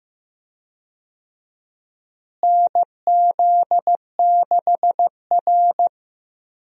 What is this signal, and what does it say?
Morse code, heard as a keyed tone, about 15 wpm, 700 Hz. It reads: NZ6R
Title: USA Call Signs - 4 Characters - Repeated 2x - 15wpm